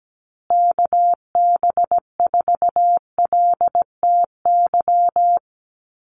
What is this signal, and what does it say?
Morse code, heard as a keyed tone, about 17 wpm, 700 Hz. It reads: KB4LTY
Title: USA Call Signs Repeated 2x 17wpm